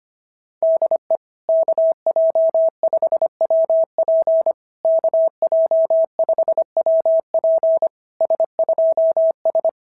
Morse code, heard as a keyed tone, about 24 wpm, 650 Hz. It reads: DE KJ5WP KJ5WP S2S
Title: SOTA QSO Practice 25wpm